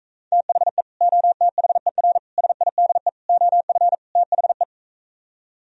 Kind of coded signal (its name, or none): Morse code